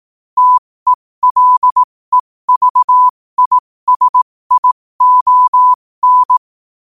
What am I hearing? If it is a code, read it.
Morse code, heard as a keyed tone, about 18 wpm, spaced as for 15 wpm, 1000 Hz. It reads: TELEVISION
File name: 15wpm